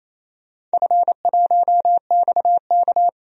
Morse code, heard as a keyed tone, about 28 wpm, 700 Hz. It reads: F1XK
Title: DXCC-100 Call Signs - 4 Characters - Repeated 2x - 28wpm